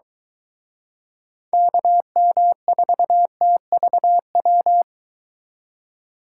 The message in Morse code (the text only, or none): KM4TVW